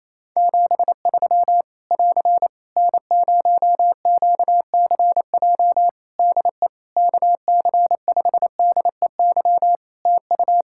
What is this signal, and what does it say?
Morse code, heard as a keyed tone, about 28 wpm, 700 Hz. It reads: 73 <AR> N0QCJ DE KC5DEY TU
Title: Practice QSO - Exchange 4 - Ending - 28wpm